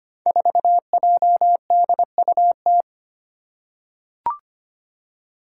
Morse code, heard as a keyed tone, about 25 wpm, 700 Hz. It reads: 4JDUT
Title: Five Challenging Letter-Number Combinations 25wpm